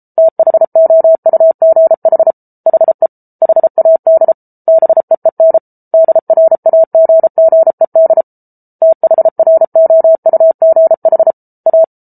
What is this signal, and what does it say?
Morse code, heard as a keyed tone, about 34 wpm, 650 Hz. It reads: THOUGH HE HAD BEEN DRAGGED THROUGH A